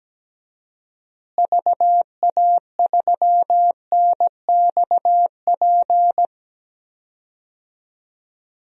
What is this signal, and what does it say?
Morse code, heard as a keyed tone, about 17 wpm, 700 Hz. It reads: VA3NXP
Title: DXCC-100 Call Signs - 6 Characters - Repeated 2x - 17wpm